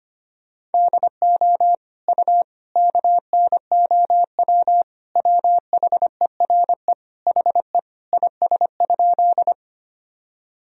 Morse code, heard as a keyed tone, about 25 wpm, 700 Hz. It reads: DO U KNOW WHERE HE IS?
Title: Sentences from Top 300 Words and Abbreviations 25wpm